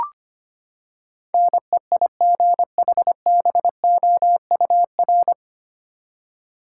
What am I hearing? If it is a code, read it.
Morse code, heard as a keyed tone, about 25 wpm, 700 Hz. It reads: NEIGHBOUR
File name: Nine Letter Words - 25wpm